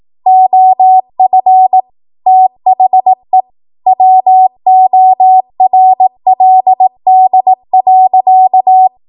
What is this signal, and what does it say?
Morse code, heard as a keyed tone, about 18 wpm, 750 Hz. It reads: OF THE WORLD.
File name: Morse / 18WPM